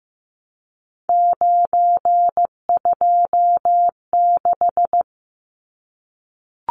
Morse code, heard as a keyed tone, about 15 wpm, 700 Hz. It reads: 926
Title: Numbers - 3-Digit 15wpm